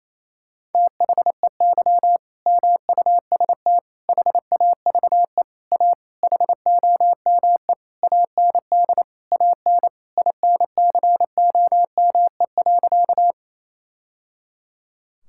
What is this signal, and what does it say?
Morse code, heard as a keyed tone, about 28 wpm, 700 Hz. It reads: THEY MUST HAVE A HOME AND AN INCOME.
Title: Sentences from Top 1000 Words - Review Part 5 - 28wpm